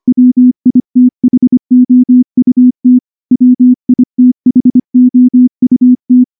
Morse code, heard as a keyed tone, about 24 wpm, 250 Hz. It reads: WITHOUT WITHOUT
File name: morse / morse-training-25WPM